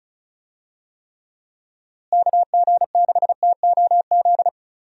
Morse code, heard as a keyed tone, about 36 wpm, 700 Hz. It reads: KG6TOZ
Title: USA Call Signs - 6 Characters - Repeated 2x - 35wpm